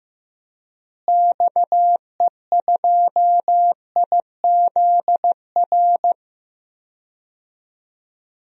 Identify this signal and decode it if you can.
Morse code, heard as a keyed tone, about 15 wpm, 700 Hz. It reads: XE2IZR
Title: DXCC-100 Call Signs - 6 Characters - Repeated 2x - 15wpm